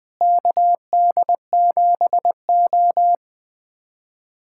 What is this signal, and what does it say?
Morse code, heard as a keyed tone, about 20 wpm, 700 Hz. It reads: KD7O
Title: USA Callsign DE Callsign - 4 Characters - 20wpm